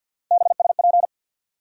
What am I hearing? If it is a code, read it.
Morse code, heard as a keyed tone, about 50 wpm, 700 Hz. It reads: BSP
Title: USA Call Sign Postfixes - 3 Letters 50wpm